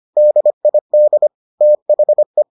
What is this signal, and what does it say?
Morse code, heard as a keyed tone, about 25 wpm, 600 Hz. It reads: DID THE